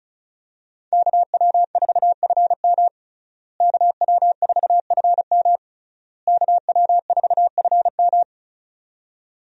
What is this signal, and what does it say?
Morse code, heard as a keyed tone, about 34 wpm, 700 Hz. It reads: KW4FM KW4FM KW4FM
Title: USA Call Signs - 5 Characters - Repeated 3x - 35wpm